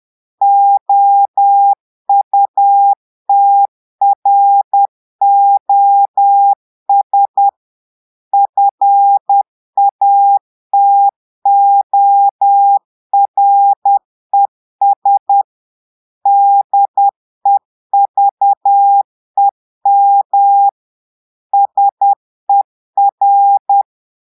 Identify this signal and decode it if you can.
Morse code, heard as a keyed tone, about 10 wpm, 800 Hz. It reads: OUTROS FATORES DEVEM SER
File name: portugais texte en clair 2 vitesse 10 mots minutes